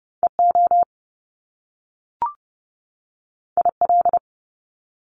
Morse code, heard as a keyed tone, about 30 wpm, 700 Hz. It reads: EO IL
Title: Top Two Letter Combinations in Words 30wpm